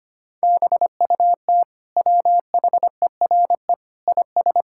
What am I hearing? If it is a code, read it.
Morse code, heard as a keyed tone, about 25 wpm, 700 Hz. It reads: BUT WHERE IS
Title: Sentences from Top 1000 Words - Review Part 5 - 25wpm